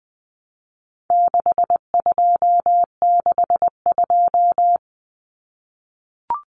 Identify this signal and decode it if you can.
Morse code, heard as a keyed tone, about 20 wpm, 700 Hz. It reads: 6262
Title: Numbers - 4-Digit 20wpm